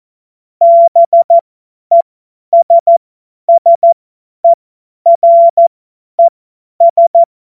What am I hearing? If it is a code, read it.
Morse code, heard as a keyed tone, about 14 wpm, spaced as for 10 wpm, 680 Hz. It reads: BESSERES